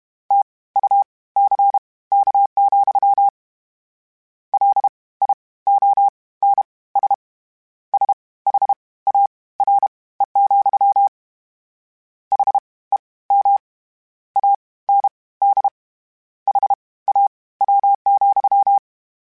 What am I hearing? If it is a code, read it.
Morse code, heard as a keyed tone, about 32 wpm, spaced as for 19 wpm, 800 Hz. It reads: TUCK, LIONS SHARE, HEM AND HAW,
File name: Misc / Misc-mp3s / WordPairs sp @30